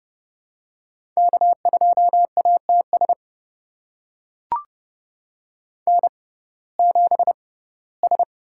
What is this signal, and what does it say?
Morse code, heard as a keyed tone, about 30 wpm, spaced as for 18 wpm, 700 Hz. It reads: K2ATS N7S
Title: USA Call Signs - 5 Characters - 6x Character Spacing - 30wpm